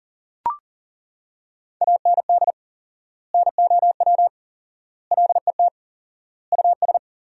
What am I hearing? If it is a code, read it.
Morse code, heard as a keyed tone, about 40 wpm, 700 Hz. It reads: AND NOW LET US
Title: 4x Word-Spacing - Sentences from Top 300 Words - 40wpm